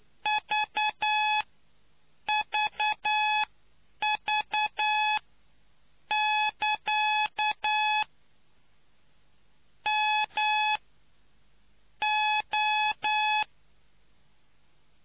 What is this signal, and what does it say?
Morse code, heard as a keyed tone, about 9 wpm, spaced as for 7 wpm, 850 Hz. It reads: VVV<KA> M O